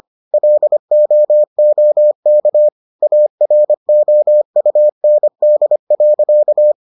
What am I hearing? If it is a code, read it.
Morse code, heard as a keyed tone, about 25 wpm, 600 Hz. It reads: LOOK AROUND.